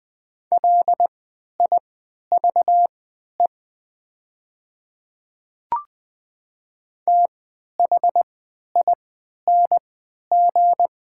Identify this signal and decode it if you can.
Morse code, heard as a keyed tone, about 20 wpm, spaced as for 11 wpm, 700 Hz. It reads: LIVE THING
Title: 3x Character Spacing - Top 200 Words - 20wpm